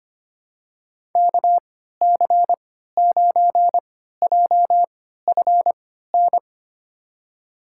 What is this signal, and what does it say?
Morse code, heard as a keyed tone, about 25 wpm, spaced as for 14 wpm, 700 Hz. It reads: KC9JFN
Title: USA Call Signs - 6 Characters - 3x Character Spacing - 25wpm